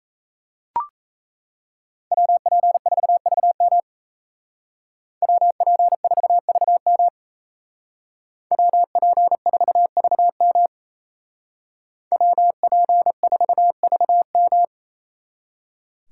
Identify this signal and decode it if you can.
Morse code, heard as a keyed tone, about 33 wpm, 700 Hz. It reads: WP4VM WP4VM WP4VM WP4VM
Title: Speed Racing - USA 2x2 Call Signs 28wpm